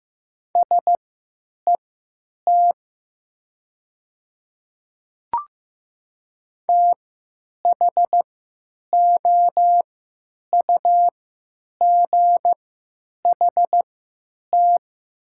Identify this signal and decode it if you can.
Morse code, heard as a keyed tone, about 15 wpm, spaced as for 9 wpm, 700 Hz. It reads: SET THOUGHT
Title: 3x Character Spacing - Top 300 Words - 15wpm